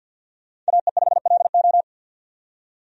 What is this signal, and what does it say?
Morse code, heard as a keyed tone, about 52 wpm, 700 Hz. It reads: AE5LO
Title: USA Call Signs - 5 Characters - 50wpm